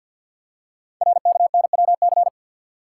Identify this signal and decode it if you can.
Morse code, heard as a keyed tone, about 48 wpm, 690 Hz. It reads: RKNWC
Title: Code Groups - 5 Letters - 50wpm